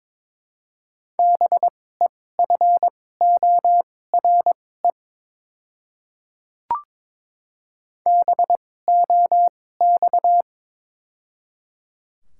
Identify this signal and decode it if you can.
Morse code, heard as a keyed tone, about 22 wpm, spaced as for 16 wpm, 700 Hz. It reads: BEFORE BOX
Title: 2x Character Spacing - Top 500 Words - 22wpm